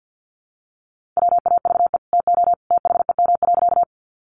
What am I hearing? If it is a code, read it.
Morse code, heard as a keyed tone, about 50 wpm, 700 Hz. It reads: WAVE TO THEM.
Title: Sentences from Top 1400 Words 50wpm